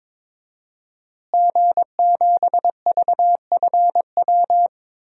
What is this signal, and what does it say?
Morse code, heard as a keyed tone, about 22 wpm, 700 Hz. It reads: G7VFW
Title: DXCC-100 Call Signs - 5 Characters - Repeated 3x - 22wpm